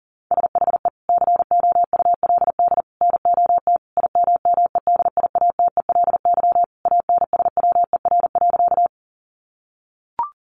Morse code, heard as a keyed tone, about 40 wpm, 700 Hz. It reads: SHE COULD NOT IMMEDIATELY ANSWER.
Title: Sentences from Top 1100 Words 40wpm